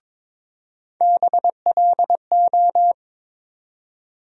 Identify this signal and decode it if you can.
Morse code, heard as a keyed tone, about 22 wpm, 700 Hz. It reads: BLO
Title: USA Call Sign Postfixes - 3 Letters 22wpm